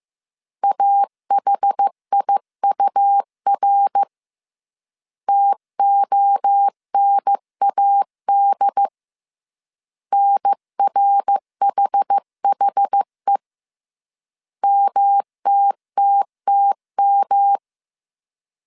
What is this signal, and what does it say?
Morse code, heard as a keyed tone, about 15 wpm, 800 Hz. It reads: AHIUR TONAD NRHHE MTTTM
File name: CW / DET / 1 3 1500C S C 14wpm 2458